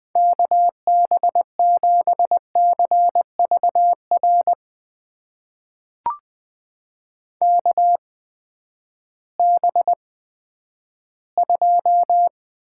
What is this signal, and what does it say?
Morse code, heard as a keyed tone, about 20 wpm, spaced as for 11 wpm, 700 Hz. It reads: KB7CVR KB2
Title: USA Call Signs - 6 Characters - 8x Character Spacing - 20wpm